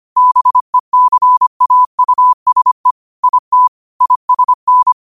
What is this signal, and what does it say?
Morse code, heard as a keyed tone, about 25 wpm, 1000 Hz. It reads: BECAUSE IT ISN